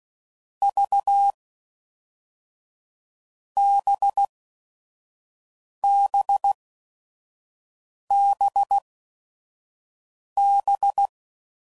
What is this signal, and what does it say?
Morse code, heard as a keyed tone, about 16 wpm, 800 Hz. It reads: V B B B B